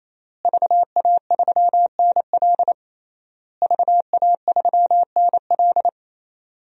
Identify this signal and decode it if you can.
Morse code, heard as a keyed tone, about 27 wpm, 700 Hz. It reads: VA3NL VA3NL
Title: DXCC-100 Call Signs - 5 Characters - Repeated 3x - 28wpm